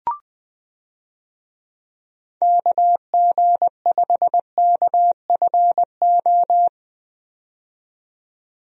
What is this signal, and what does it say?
Morse code, heard as a keyed tone, about 20 wpm, 700 Hz. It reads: KG5KFO
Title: USA Call Signs - 6 Characters - 20wpm